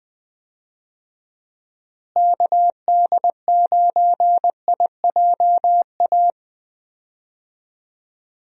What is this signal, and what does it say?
Morse code, heard as a keyed tone, about 20 wpm, 700 Hz. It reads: KD9IJA
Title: USA Call Signs - 6 Characters - Repeated 3x - 20wpm